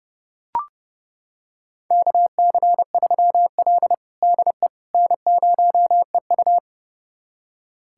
Morse code, heard as a keyed tone, about 32 wpm, 700 Hz. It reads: KC3L DE N0EU
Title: USA Callsign DE Callsign - 4 Characters - 30wpm